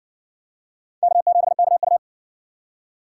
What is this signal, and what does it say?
Morse code, heard as a keyed tone, about 60 wpm, 700 Hz. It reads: K7KU